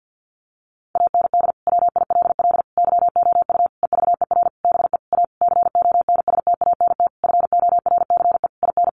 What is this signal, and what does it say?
Morse code, heard as a keyed tone, about 50 wpm, 700 Hz. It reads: AND WILL YOU EVER BE A CONSTANT FORCE IN
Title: Sentences from Top 1500 Words 50wpm